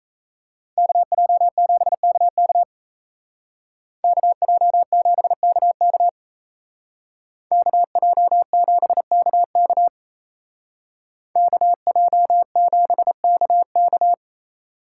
Morse code, heard as a keyed tone, about 36 wpm, 700 Hz. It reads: KJ7KK KJ7KK KJ7KK KJ7KK